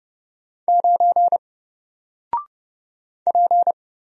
Morse code, heard as a keyed tone, about 31 wpm, 700 Hz. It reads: 9 P
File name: Single-Letter-Number-30wpm